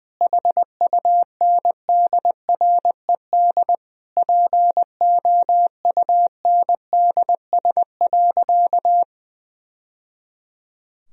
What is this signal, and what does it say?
Morse code, heard as a keyed tone, about 20 wpm, 700 Hz. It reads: HUNDRED POUNDS.
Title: Sentences from Top 1000 Words - Review Part 4 - 20wpm